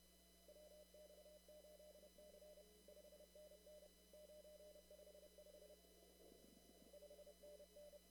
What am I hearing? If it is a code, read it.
Morse code, heard as a keyed tone, about 31 wpm, 600 Hz. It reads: WX8C 5NN 955 5NN
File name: CW / kt5j wpx cw-op-k5pi